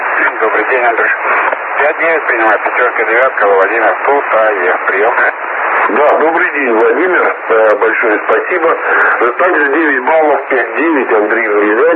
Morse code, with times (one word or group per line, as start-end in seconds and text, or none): none